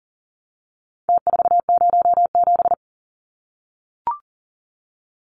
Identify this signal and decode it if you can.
Morse code, heard as a keyed tone, about 40 wpm, 700 Hz. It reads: T407
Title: Numbers - Contest Style - 4-Digit 40wpm